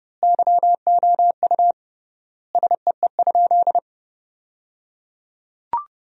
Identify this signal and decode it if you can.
Morse code, heard as a keyed tone, about 30 wpm, 700 Hz. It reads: YOU SEE?
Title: 3x Word-Spacing - Sentences from Top 100 Words - 30wpm